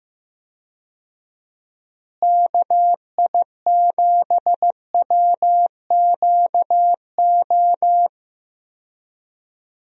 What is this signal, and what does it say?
Morse code, heard as a keyed tone, about 15 wpm, 700 Hz. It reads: KI7WQO
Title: USA Call Signs - 6 Characters - Repeated 3x - 15wpm